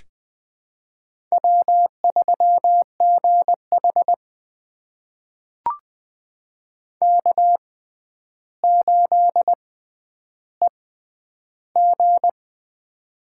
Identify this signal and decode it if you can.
Morse code, heard as a keyed tone, about 20 wpm, spaced as for 10 wpm, 700 Hz. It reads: W3GH K8EG
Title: USA Call Signs - 4 Characters - 6x Character Spacing - 20wpm